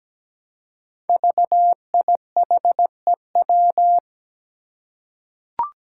Morse code, heard as a keyed tone, about 17 wpm, 700 Hz. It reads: VIHEW
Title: Code Groups - 5 Letter:Numbers - 17wpm